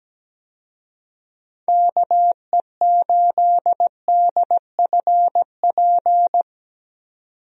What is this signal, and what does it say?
Morse code, heard as a keyed tone, about 17 wpm, 700 Hz. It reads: KE8DFP
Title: USA Call Signs - 6 Characters - 17wpm